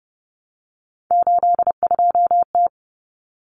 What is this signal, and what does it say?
Morse code, heard as a keyed tone, about 30 wpm, 700 Hz. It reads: 82T
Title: Common Cut Numbers - 3-Digit 30wpm